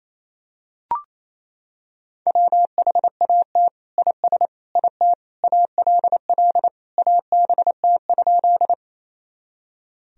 Morse code, heard as a keyed tone, about 27 wpm, 700 Hz. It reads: WHAT IS IT ALL ABT?
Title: Sentences from Top 100 Words and Abbreviations 28wpm